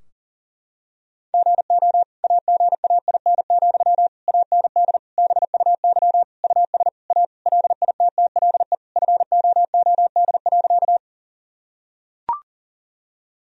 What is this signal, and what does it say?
Morse code, heard as a keyed tone, about 40 wpm, 700 Hz. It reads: GO AGAIN, AND BUY US A LITTLE FOOD.
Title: Sentences from Top 1000 Words - Review Part 4 - 40wpm